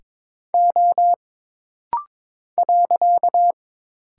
Morse code, heard as a keyed tone, about 22 wpm, 700 Hz. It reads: O .
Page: This rapid-fire timing is geared to push your ICR (Instant Character Recognition) skills.